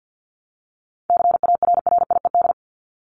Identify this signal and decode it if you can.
Morse code, heard as a keyed tone, about 50 wpm, 700 Hz. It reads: CARRIED
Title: ED Suffix - Top 100 Words with Base Words 50wpm